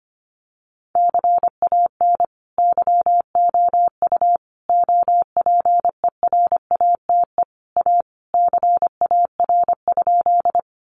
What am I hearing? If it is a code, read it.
Morse code, heard as a keyed tone, about 25 wpm, 700 Hz. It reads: CAN YOU OPERATE A CAR?
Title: Sentences from Top 1500 Words 25wpm